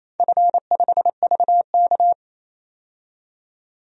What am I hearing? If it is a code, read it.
Morse code, heard as a keyed tone, about 28 wpm, 700 Hz. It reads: F5VK
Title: DXCC-100 Call Signs - 4 Characters - 28wpm